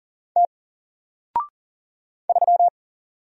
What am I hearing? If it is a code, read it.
Morse code, heard as a keyed tone, about 40 wpm, 700 Hz. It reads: T 3
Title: Single Letter-Number 40wpm